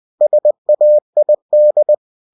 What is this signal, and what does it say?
Morse code, heard as a keyed tone, about 20 wpm, 600 Hz. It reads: SAID
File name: cw-books / winnie-the-pooh / winnie-20wpm-page-0043